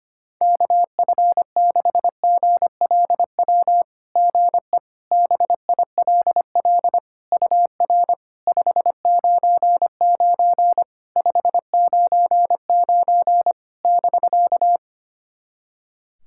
Morse code, heard as a keyed tone, about 25 wpm, 700 Hz. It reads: KF6GLW GE BILL UR 599 599 <BK>
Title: SOTA QSO Practice 25wpm